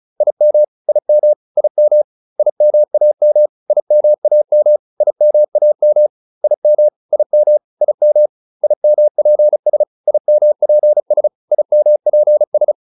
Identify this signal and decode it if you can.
Morse code, heard as a keyed tone, about 34 wpm, 600 Hz. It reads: IM IM IM IMAM IMAM IMAM IM IM IM IMPS IMPS IMPS